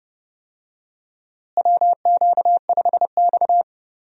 Morse code, heard as a keyed tone, about 29 wpm, 700 Hz. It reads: WQ5X
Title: USA Call Signs - 4 Characters - Repeated 2x - 30wpm